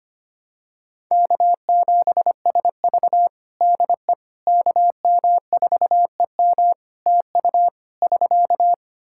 Morse code, heard as a keyed tone, about 25 wpm, 700 Hz. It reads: K7SV DE KM4EM TU <SK>
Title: Practice QSO - Exchange 4 - Ending - 25wpm